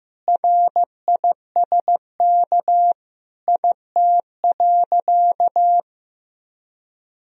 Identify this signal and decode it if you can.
Morse code, heard as a keyed tone, about 15 wpm, 700 Hz. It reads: RISK IT.
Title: Sentences from Top 1000 Words - Review Part 4 - 15wpm